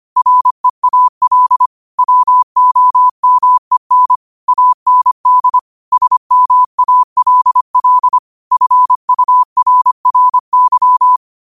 Morse code, heard as a keyed tone, about 25 wpm, 1000 Hz. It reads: REAL WOMEN AND SMALL FURRY
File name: qotdfiles / 25wpm